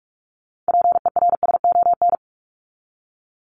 Morse code, heard as a keyed tone, about 44 wpm, 700 Hz. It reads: PERSON
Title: Top 500 Words 45wpm